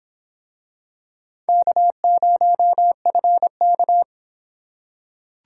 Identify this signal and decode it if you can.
Morse code, heard as a keyed tone, about 24 wpm, 700 Hz. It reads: K0FK